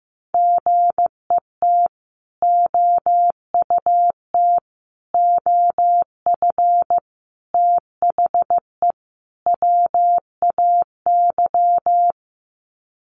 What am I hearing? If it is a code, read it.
Morse code, heard as a keyed tone, about 15 wpm, 700 Hz. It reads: GET OUT OF THE WAY
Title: Sets of 5 Words Encore 15wpm